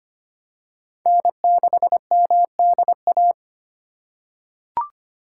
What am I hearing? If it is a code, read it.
Morse code, heard as a keyed tone, about 25 wpm, 700 Hz. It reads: N6MDA
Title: USA Call Signs - 5 Characters - 3x Character Spacing - 25wpm